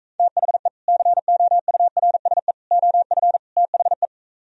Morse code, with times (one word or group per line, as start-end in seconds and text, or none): 0.2-0.7 THE
0.9-2.5 COURSE
2.7-3.4 OF
3.6-4.1 THE